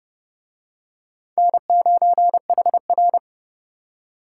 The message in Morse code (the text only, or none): N9HR